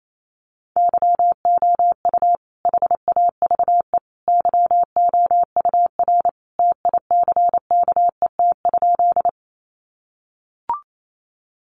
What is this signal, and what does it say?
Morse code, heard as a keyed tone, about 28 wpm, 700 Hz. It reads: YOU HAVE YOUR TICKET?
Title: Sentences from Top 1400 Words 28wpm